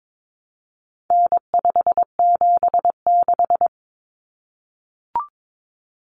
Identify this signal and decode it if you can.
Morse code, heard as a keyed tone, about 22 wpm, 700 Hz. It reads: N576
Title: Common Cut Numbers - 4-Digit 22wpm